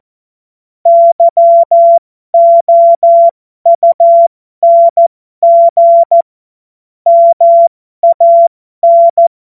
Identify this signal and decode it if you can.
Morse code, heard as a keyed tone, about 14 wpm, spaced as for 12 wpm, 680 Hz. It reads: YOUNG MAN
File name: Morsekurs / Witze